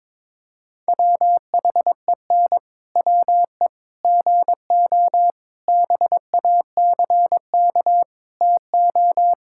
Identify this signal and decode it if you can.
Morse code, heard as a keyed tone, about 21 wpm, 700 Hz. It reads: WHEN WE GO BACK TO
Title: Pangram Challenge v2 - 22wpm